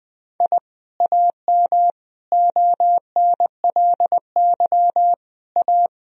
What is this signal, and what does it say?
Morse code, heard as a keyed tone, about 20 wpm, 700 Hz. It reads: I AM ONLY A
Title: Sentences from Top 1000 Words - Review Part 2 - 20wpm